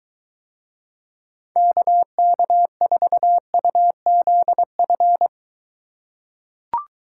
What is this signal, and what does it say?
Morse code, heard as a keyed tone, about 22 wpm, 700 Hz. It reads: KK4UZF